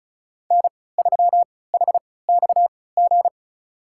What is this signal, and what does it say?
Morse code, heard as a keyed tone, about 34 wpm, spaced as for 20 wpm, 700 Hz. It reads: N3HXG
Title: USA Call Signs - 5 Characters - 3x Character Spacing - 35wpm